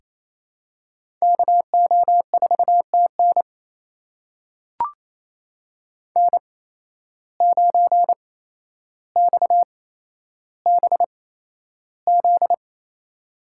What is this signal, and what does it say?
Morse code, heard as a keyed tone, about 28 wpm, spaced as for 12 wpm, 700 Hz. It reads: KO4TN N9XBZ
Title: USA Call Signs - 5 Characters - 8x Character Spacing - 28wpm